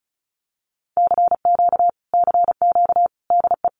Morse code, heard as a keyed tone, about 34 wpm, 700 Hz. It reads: CQ CQ DE
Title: Common QSO Phrases Encore 35wpm